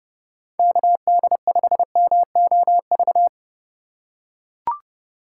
Morse code, heard as a keyed tone, about 30 wpm, 700 Hz. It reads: KD5MOV
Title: USA Call Signs - 6 Characters - 4x Character Spacing - 30wpm